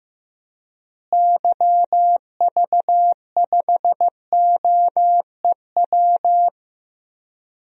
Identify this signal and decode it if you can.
Morse code, heard as a keyed tone, about 15 wpm, 700 Hz. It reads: YV5OEW
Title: DXCC-100 Call Signs - 6 Characters - Repeated 2x - 15wpm